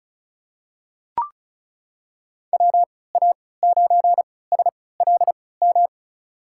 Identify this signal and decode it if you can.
Morse code, heard as a keyed tone, about 36 wpm, spaced as for 20 wpm, 700 Hz. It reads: WA9SLM
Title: USA Call Signs - 3x Character Spacing - 35wpm